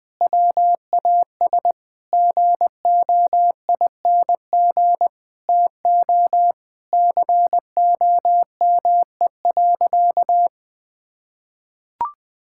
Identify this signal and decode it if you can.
Morse code, heard as a keyed tone, about 20 wpm, 700 Hz. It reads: WAS GOING TO COME.
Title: Sentences with 5-Word Phrases - Spoken Beginning - 20wpm